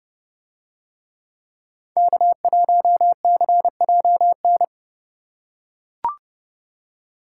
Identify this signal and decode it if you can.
Morse code, heard as a keyed tone, about 29 wpm, 700 Hz. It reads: K1CJN